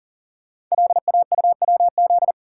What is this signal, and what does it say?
Morse code, heard as a keyed tone, about 38 wpm, 690 Hz. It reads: LAUWZ
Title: Code Groups - 5 Letter:Numbers - 40wpm